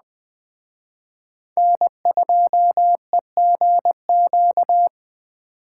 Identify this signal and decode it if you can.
Morse code, heard as a keyed tone, about 19 wpm, 700 Hz. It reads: N2EGQ